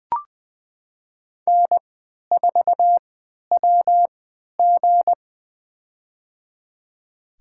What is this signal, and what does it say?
Morse code, heard as a keyed tone, about 20 wpm, spaced as for 11 wpm, 700 Hz. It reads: N4WG
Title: USA Call Signs - 3x Character Spacing - 20wpm